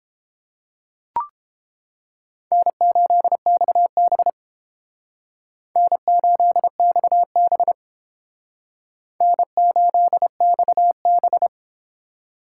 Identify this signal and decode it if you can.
Morse code, heard as a keyed tone, about 32 wpm, 700 Hz. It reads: N8XB N8XB N8XB